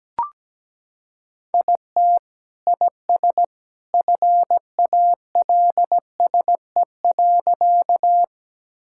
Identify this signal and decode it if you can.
Morse code, heard as a keyed tone, about 17 wpm, 700 Hz. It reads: IT IS FALSE.
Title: Sentences from Top 1600 Words 17wpm